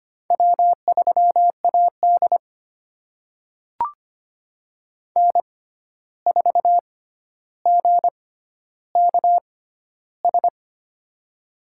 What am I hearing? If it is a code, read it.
Morse code, heard as a keyed tone, about 25 wpm, spaced as for 12 wpm, 700 Hz. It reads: W3AD N4GKS
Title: USA Call Signs - 6x Character Spacing - 25wpm